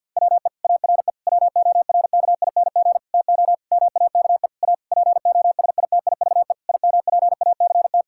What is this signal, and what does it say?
Morse code, heard as a keyed tone, about 50 wpm, 700 Hz. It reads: WE ARE WORKING TO MAKE A POSITIVE IMPACT
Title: Sentences from Top 1000 Words - Review Part 6 - 50wpm